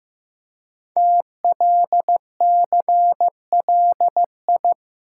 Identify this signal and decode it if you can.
Morse code, heard as a keyed tone, about 15 wpm, 700 Hz. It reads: TLCLI
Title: Code Groups - 5 Letter:Numbers - 15wpm